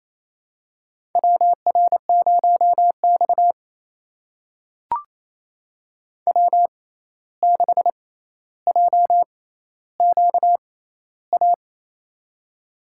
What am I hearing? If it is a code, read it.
Morse code, heard as a keyed tone, about 27 wpm, spaced as for 13 wpm, 700 Hz. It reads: WR0X W6JQA
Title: USA Call Signs - 6x Character Spacing - 28wpm